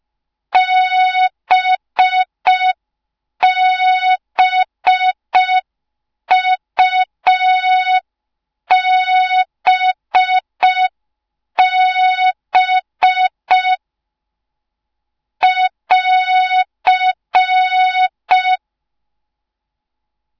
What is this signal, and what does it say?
Morse code, heard as a keyed tone, about 5 wpm, 3700 Hz. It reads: BBUBB <AR>